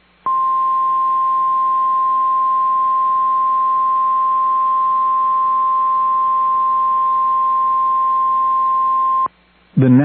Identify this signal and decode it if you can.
Emergency Alert System (SAME) tones (the 1050 Hz attention tone). It is heard from 0.2 to 9.3 s.